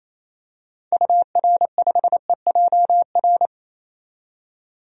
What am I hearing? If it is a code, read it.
Morse code, heard as a keyed tone, about 28 wpm, 700 Hz. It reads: UR5EJR
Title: DXCC-100 Call Signs - 6 Characters - 28wpm